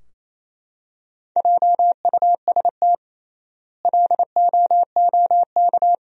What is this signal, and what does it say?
Morse code, heard as a keyed tone, about 28 wpm, 700 Hz. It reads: JUST LOOK
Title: 3x Word-Spacing - Sentences from Top 200 Words - 28wpm